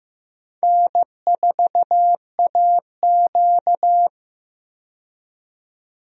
Morse code, heard as a keyed tone, about 15 wpm, 700 Hz. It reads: N4AQ
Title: USA Call Signs - 4 Characters - 15wpm